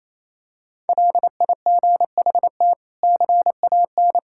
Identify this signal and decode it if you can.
Morse code, heard as a keyed tone, about 28 wpm, 700 Hz. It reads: LIGHT CAN
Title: Sentences from Top 1000 Words - Review Part 6 - 28wpm